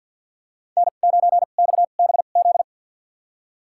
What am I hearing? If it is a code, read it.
Morse code, heard as a keyed tone, about 49 wpm, spaced as for 35 wpm, 700 Hz. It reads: N9XBZ
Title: USA Call Signs - 2x Character Spacing - 50wpm